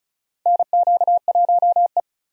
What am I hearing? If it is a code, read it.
Morse code, heard as a keyed tone, about 36 wpm, 700 Hz. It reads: NQ1E
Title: USA Call Signs - 4 Characters - 35wpm